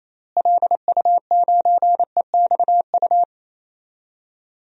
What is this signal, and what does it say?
Morse code, heard as a keyed tone, about 27 wpm, 700 Hz. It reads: LU9EXU
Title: DXCC-100 Call Signs - 6 Characters - Repeated 3x - 28wpm